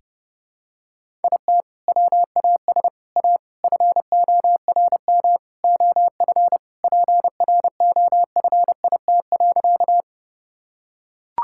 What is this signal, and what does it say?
Morse code, heard as a keyed tone, about 32 wpm, 700 Hz. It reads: IT WAS A FORM OF PROFIT.
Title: Sentences from Top 1000 Words - Review Part 5 - 30wpm